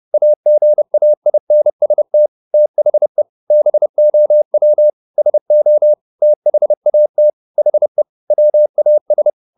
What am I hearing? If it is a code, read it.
Morse code, heard as a keyed tone, about 30 wpm, 600 Hz. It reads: AGAINST THE BOW SO THAT HE WAS
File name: the old man and the sea / the old man and the sea 1 30wpm